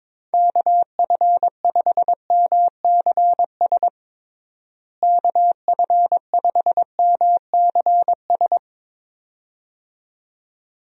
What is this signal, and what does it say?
Morse code, heard as a keyed tone, about 22 wpm, 700 Hz. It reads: KF5MCS KF5MCS
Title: USA Call Signs - 6 Characters - Repeated 2x - 22wpm